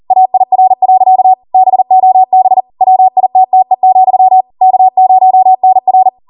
Morse code, heard as a keyed tone, about 40 wpm, 750 Hz. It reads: AIR. BOB WITTE, K0NR